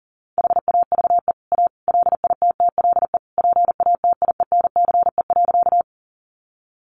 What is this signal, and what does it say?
Morse code, heard as a keyed tone, about 40 wpm, 700 Hz. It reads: HAVE A LITTLE PATIENCE.
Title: Sentences from Top 1300 Words 40wpm